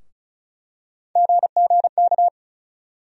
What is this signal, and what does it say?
Morse code, heard as a keyed tone, about 36 wpm, 700 Hz. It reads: GGK
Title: USA Call Sign Postfixes - 3 Letters 35wpm